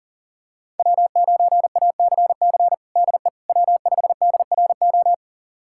Morse code, heard as a keyed tone, about 40 wpm, 700 Hz. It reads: W9ACC DE W5DRO
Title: USA Callsign DE Callsign - 5 Characters - 40wpm